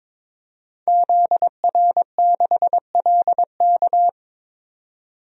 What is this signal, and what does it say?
Morse code, heard as a keyed tone, about 22 wpm, 700 Hz. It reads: ZR6LK
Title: DXCC-100 Call Signs - 5 Characters - 22wpm